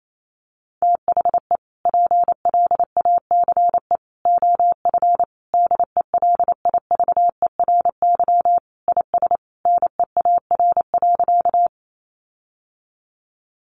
Morse code, heard as a keyed tone, about 28 wpm, 700 Hz. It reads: THE PLACE OF DELIVERY IS NEAR.
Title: Sentences from Top 1500 Words 28wpm